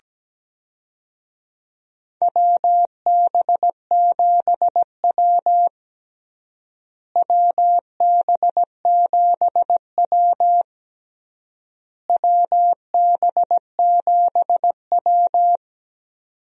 Morse code, heard as a keyed tone, about 17 wpm, 700 Hz. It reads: WB7W WB7W WB7W
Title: USA Call Signs - 4 Characters - Repeated 3x - 17wpm